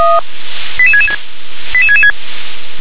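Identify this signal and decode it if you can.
Five-tone selective call (ZVEI), heard as quick runs of five tones, tones of about 70 ms. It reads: 88666 88666
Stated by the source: ZVEI five-tone selective call, received by SDR.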